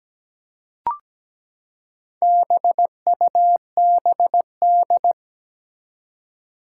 Morse code, heard as a keyed tone, about 17 wpm, 700 Hz. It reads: BUBD
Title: Four Challenging Letter-Number Combinations 17wpm